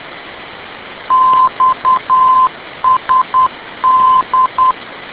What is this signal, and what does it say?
Morse code, heard as a keyed tone, about 10 wpm, 1030 Hz. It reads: XSD
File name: info / xsd ndb